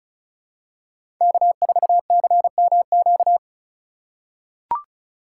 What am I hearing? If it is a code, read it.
Morse code, heard as a keyed tone, about 36 wpm, 700 Hz. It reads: K4CMQ
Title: USA Call Signs - 5 Characters - 4x Character Spacing - 35wpm